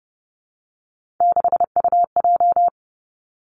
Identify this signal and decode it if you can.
Morse code, heard as a keyed tone, about 30 wpm, 700 Hz. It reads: 6UJ
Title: Three Challenging Letter-Number Combinations 30wpm